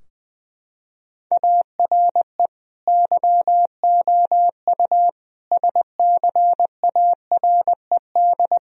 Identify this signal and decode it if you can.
Morse code, heard as a keyed tone, about 21 wpm, 700 Hz. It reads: ARE YOU SCARED
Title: Sentences from Top 1000 Words - Review Part 4 - 20wpm